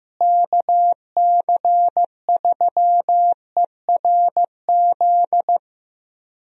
Morse code, heard as a keyed tone, about 15 wpm, 700 Hz. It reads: KC3ERZ
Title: USA Call Signs - 6 Characters - Repeated 3x - 15wpm